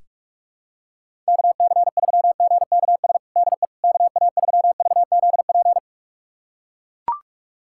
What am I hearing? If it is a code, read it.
Morse code, heard as a keyed tone, about 41 wpm, 700 Hz. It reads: KC3GKS DE KA3VZP
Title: USA Callsign DE Callsign - 6 Characters - 45wpm